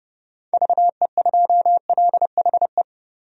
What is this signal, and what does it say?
Morse code, heard as a keyed tone, about 30 wpm, 700 Hz. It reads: VE2LHE
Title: DXCC-100 Call Signs - 6 Characters - 30wpm